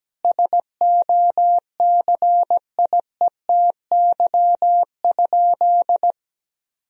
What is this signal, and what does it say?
Morse code, heard as a keyed tone, about 17 wpm, 700 Hz. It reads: SOCIETY?
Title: Sentences from Top 1000 Words - Review Part 4 - 17wpm